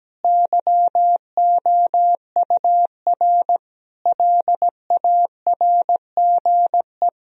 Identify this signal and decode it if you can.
Morse code, heard as a keyed tone, about 17 wpm, 700 Hz. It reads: YOUR LARGE
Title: Sentences from Top 1000 Words - Review Part 1 - 17wpm